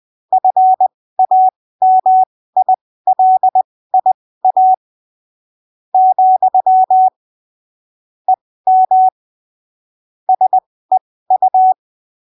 Morse code, heard as a keyed tone, about 20 wpm, spaced as for 15 wpm, 750 Hz. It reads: FAMILIA , EM SEU
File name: arquivos / texto / texto 28 20ppm